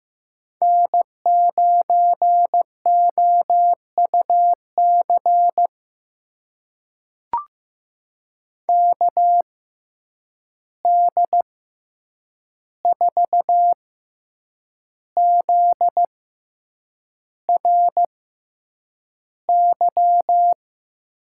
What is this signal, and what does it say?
Morse code, heard as a keyed tone, about 15 wpm, spaced as for 7 wpm, 700 Hz. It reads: N9OUC KD4ZRY
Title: USA Call Signs - 6x Character Spacing - 15wpm